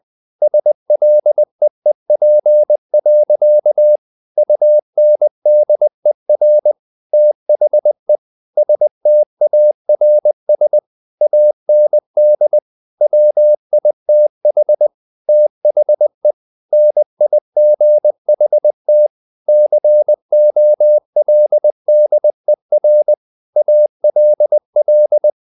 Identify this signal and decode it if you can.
Morse code, heard as a keyed tone, about 20 wpm, 600 Hz. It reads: SLEEP. UNDER THE STARS AND WITH THE NIGHT COLDER ALL
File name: the old man and the sea part 2 20wpm